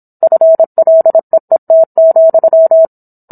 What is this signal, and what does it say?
Morse code, heard as a keyed tone, about 27 wpm, 650 Hz. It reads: FLEET,